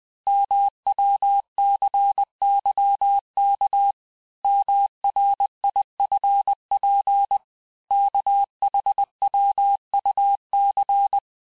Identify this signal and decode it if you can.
Morse code, heard as a keyed tone, about 20 wpm, 800 Hz. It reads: MWCYK MRIFP KHWUC
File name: cours-858-texte-chiffre-vitesse-19mpm